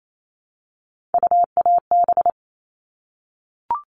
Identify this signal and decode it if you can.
Morse code, heard as a keyed tone, about 28 wpm, 700 Hz. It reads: UAB
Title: All Cut Numbers - 3-Digit 28wpm